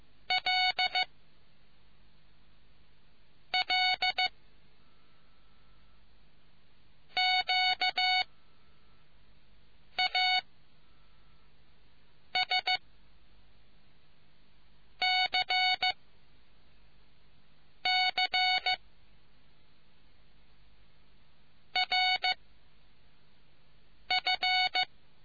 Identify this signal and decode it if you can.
Morse code, heard as a keyed tone, about 15 wpm, 740 Hz. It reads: L L Q A S C C R F